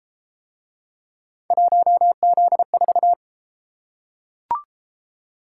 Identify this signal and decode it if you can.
Morse code, heard as a keyed tone, about 29 wpm, 700 Hz. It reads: 1Z4